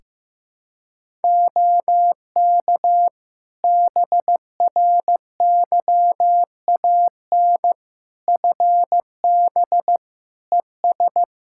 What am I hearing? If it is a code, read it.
Morse code, heard as a keyed tone, about 15 wpm, 700 Hz. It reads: OK BRYAN FB ES
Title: Practice QSO - Exchange 2 - Info - 15wpm